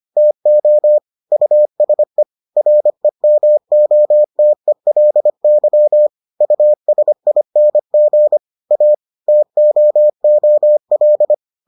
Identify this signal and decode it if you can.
Morse code, heard as a keyed tone, about 25 wpm, 600 Hz. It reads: TO USE REMOTELY USING A TOOL